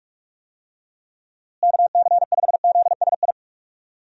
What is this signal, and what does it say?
Morse code, heard as a keyed tone, about 46 wpm, 700 Hz. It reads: KC5ZSI
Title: USA Call Signs - 6 Characters - Repeated 2x - 45wpm